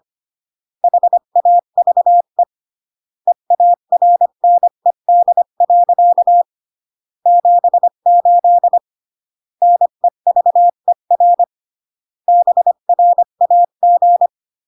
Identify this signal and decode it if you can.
Morse code, heard as a keyed tone, about 25 wpm, 700 Hz. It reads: HAVE EARNED. 78 NEVER BRAG